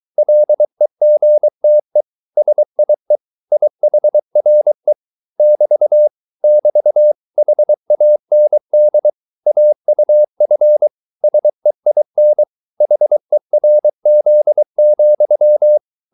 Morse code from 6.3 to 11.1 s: <BT> HAND AUF